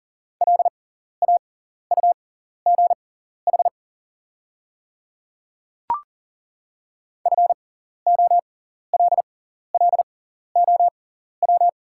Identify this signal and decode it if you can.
Morse code, heard as a keyed tone, about 40 wpm, spaced as for 14 wpm, 700 Hz. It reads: LAUGH FOLLOW
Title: 6x Character Spacing - Top 500 Words - 40wpm